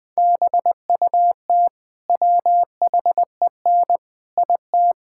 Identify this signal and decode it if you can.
Morse code, heard as a keyed tone, about 20 wpm, 700 Hz. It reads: BUT WHEN IT
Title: Build A Phrase - 5-Word Phrases - 20wpm